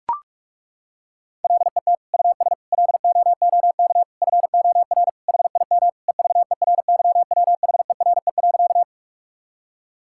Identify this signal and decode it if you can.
Morse code, heard as a keyed tone, about 41 wpm, 700 Hz. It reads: LET US LOOK FOR HIM EVERYWHERE.
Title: Sentences from Top 1600 Words 45wpm